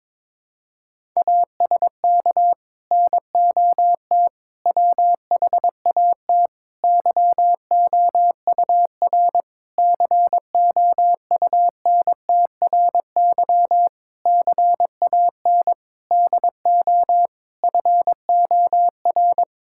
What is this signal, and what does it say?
Morse code, heard as a keyed tone, about 22 wpm, 700 Hz. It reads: ASK NOT WHAT YOUR COUNTRY CAN DO FOR
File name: practice / 22wpm / Sets of 8 Words - 22wpm